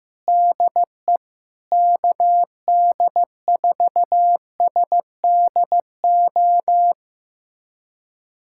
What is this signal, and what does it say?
Morse code, heard as a keyed tone, about 16 wpm, 700 Hz. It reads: DE KD4SDO
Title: USA Callsign DE Callsign 15wpm